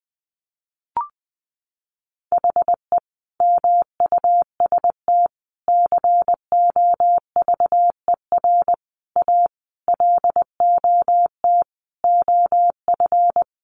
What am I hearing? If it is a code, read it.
Morse code, heard as a keyed tone, about 20 wpm, 700 Hz. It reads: HE MUST COVER A LOT OF
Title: Sentences from Top 500 Words 20wpm